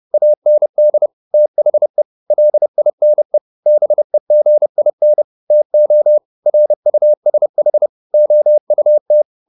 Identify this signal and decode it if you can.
Morse code, heard as a keyed tone, about 30 wpm, 600 Hz. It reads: AND THE LINE BEGIN TO RUSH OUT
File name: the old man and the sea / the old man and the sea 1 30wpm